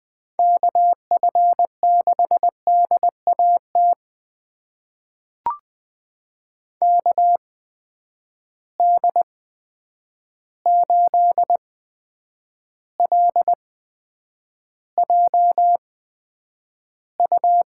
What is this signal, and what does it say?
Morse code, heard as a keyed tone, about 20 wpm, spaced as for 9 wpm, 700 Hz. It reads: KF6DAT KD8LJU
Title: USA Call Signs - 6 Characters - 8x Character Spacing - 20wpm